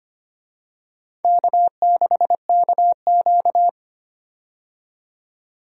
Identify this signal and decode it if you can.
Morse code, heard as a keyed tone, about 25 wpm, 700 Hz. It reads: K6KQ
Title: USA Call Signs - 4 Characters - 25wpm